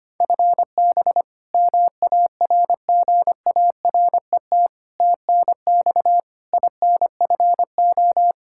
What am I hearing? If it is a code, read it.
Morse code, heard as a keyed tone, about 25 wpm, 700 Hz. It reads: FB MARGARET TNX INFO
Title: Practice QSO - Exchange 3 - More Info - 25wpm